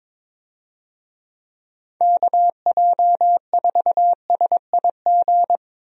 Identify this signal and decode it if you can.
Morse code, heard as a keyed tone, about 22 wpm, 700 Hz. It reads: KJ4SIG
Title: USA Call Signs - 6 Characters - 22wpm